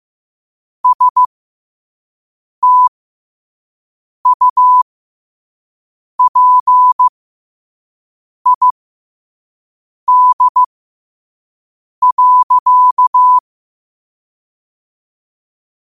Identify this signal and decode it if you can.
Morse code, heard as a keyed tone, about 15 wpm, spaced as for 5 wpm, 1000 Hz. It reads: STUPID.
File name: qotdfiles / 5wpm